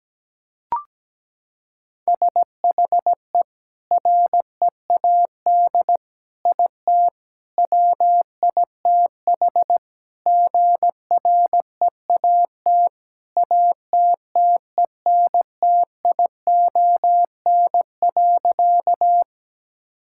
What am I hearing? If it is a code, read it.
Morse code, heard as a keyed tone, about 17 wpm, 700 Hz. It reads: SHE READ IT WITH GREAT ATTENTION.
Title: Sentences from Top 1000 Words - Review Part 6 - 17wpm